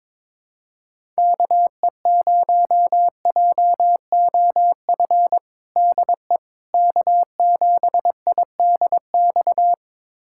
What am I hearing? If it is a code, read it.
Morse code, heard as a keyed tone, about 22 wpm, 700 Hz. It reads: KE0JOF DE K7IDX
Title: USA Callsign DE Callsign 22wpm